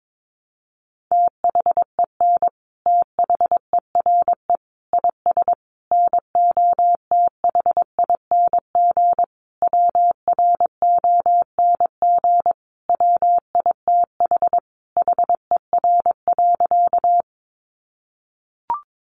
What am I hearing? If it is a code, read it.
Morse code, heard as a keyed tone, about 22 wpm, 700 Hz. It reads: THEN THERE IS NOTHING WRONG WITH HER.
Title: Sentences from Top 900 Words 22wpm